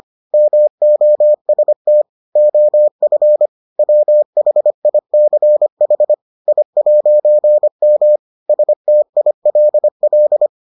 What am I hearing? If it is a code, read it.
Morse code, heard as a keyed tone, about 25 wpm, 600 Hz. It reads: MOST OF WHICH I'M STILL